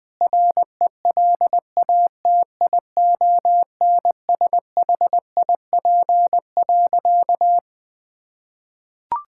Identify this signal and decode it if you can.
Morse code, heard as a keyed tone, about 20 wpm, 700 Hz. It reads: RELATIONSHIP.
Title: Sentences with 5-Word Phrases - 20wpm